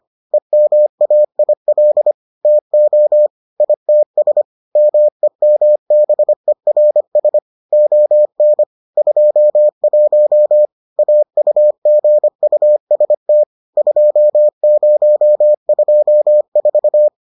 Morse code, heard as a keyed tone, about 25 wpm, 600 Hz. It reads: EMAIL TO ITS MEMBERS ON 21 AUGUST 2024